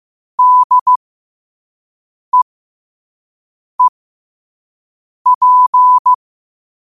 Morse code, heard as a keyed tone, about 15 wpm, spaced as for 5 wpm, 1000 Hz. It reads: DEEP
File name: qotdfiles / 5wpm